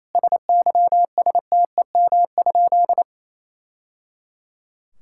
Morse code, heard as a keyed tone, about 28 wpm, 700 Hz. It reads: SYSTEM?
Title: Sentences from Top 1000 Words - Review Part 5 - 28wpm